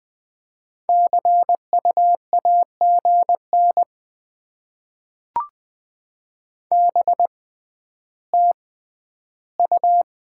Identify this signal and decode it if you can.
Morse code, heard as a keyed tone, about 20 wpm, spaced as for 12 wpm, 700 Hz. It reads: CUAGN BTU
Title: Top 20 QSO Elements - 6x Character Spacing - 20wpm